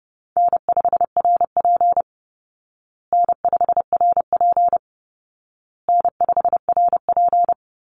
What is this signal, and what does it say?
Morse code, heard as a keyed tone, about 30 wpm, 700 Hz. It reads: N5RP N5RP N5RP
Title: Easy Call Signs Repeated 3x - 30wpm